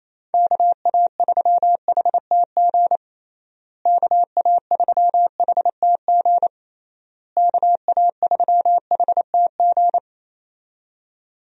Morse code, heard as a keyed tone, about 28 wpm, 700 Hz. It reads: KA3HTG KA3HTG KA3HTG
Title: USA Call Signs - 6 Characters - Repeated 3x - 28wpm